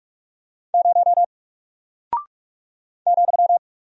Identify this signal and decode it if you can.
Morse code, heard as a keyed tone, about 44 wpm, 700 Hz. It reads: 0 ,
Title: Single Character (Letter:Number:Punctuation:Prosign) Alt 45wpm